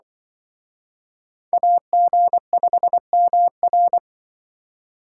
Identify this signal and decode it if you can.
Morse code, heard as a keyed tone, about 24 wpm, 700 Hz. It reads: AG5MR
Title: Speed Racing - USA 2x2 Call Signs 20wpm